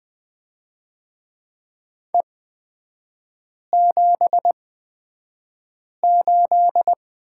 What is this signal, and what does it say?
Morse code, heard as a keyed tone, about 20 wpm, spaced as for 5 wpm, 700 Hz. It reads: E78
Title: 243 Lesson 39e - Letter z - Three Characters 20wpm